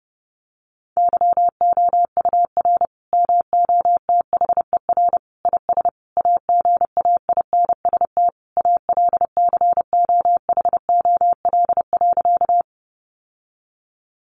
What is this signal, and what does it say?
Morse code, heard as a keyed tone, about 28 wpm, 700 Hz. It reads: YOUR MOTHER IS AGAINST ALCOHOL.
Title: Sentences from Top 1400 Words 30wpm